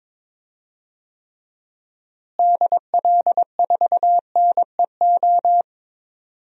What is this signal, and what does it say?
Morse code, heard as a keyed tone, about 21 wpm, 700 Hz. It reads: DL4NEO
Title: DXCC-100 Call Signs - 6 Characters - Repeated 2x - 22wpm